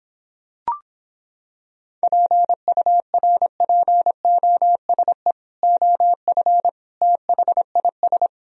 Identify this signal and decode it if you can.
Morse code, heard as a keyed tone, about 26 wpm, 700 Hz. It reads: PURPOSE OF THIS